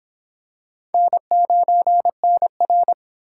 Morse code, heard as a keyed tone, about 21 wpm, 700 Hz. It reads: N9NR
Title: Speed Racing - USA 1x2 Call Signs 17wpm